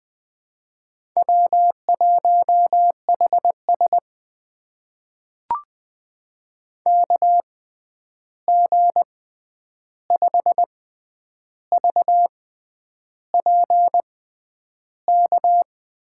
Morse code, heard as a keyed tone, about 20 wpm, spaced as for 9 wpm, 700 Hz. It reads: W1HS KG5VPK
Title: USA Call Signs - 6x Character Spacing - 20wpm